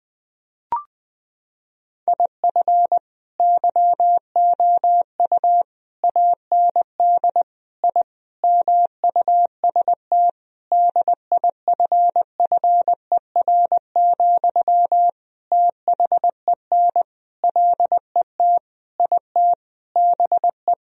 Morse code, heard as a keyed tone, about 21 wpm, 700 Hz. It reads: IF YOU AND I MUST DIFFER, THEN LET IT BE
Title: Sentences from Top 1000 Words - Review Part 1 - 20wpm